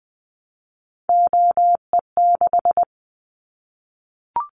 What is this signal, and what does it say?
Morse code, heard as a keyed tone, about 20 wpm, 700 Hz. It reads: OE6
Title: Callsign Prefixes for Top 25 DXCC - 20wpm